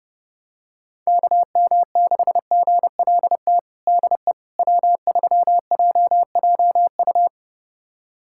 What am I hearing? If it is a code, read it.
Morse code, heard as a keyed tone, about 30 wpm, 700 Hz. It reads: KM6GLT DE W3JJU
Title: USA Callsign DE Callsign 30wpm